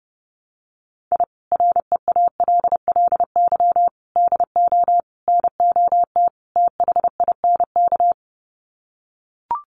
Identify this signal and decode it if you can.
Morse code, heard as a keyed tone, about 30 wpm, 700 Hz. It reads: I REALLY DO NOT THINK
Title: Sets of 5 Words Encore2 30wpm